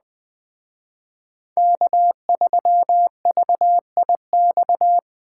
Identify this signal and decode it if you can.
Morse code, heard as a keyed tone, about 19 wpm, 700 Hz. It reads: K3VIX